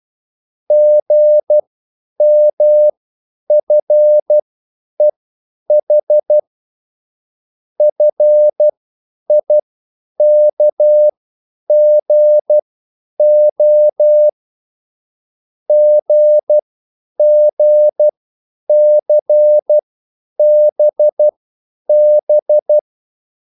Morse code, heard as a keyed tone, about 12 wpm, spaced as for 9 wpm, 600 Hz. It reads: GMFEH FIKGO GGCBB
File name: QSL / CW lesson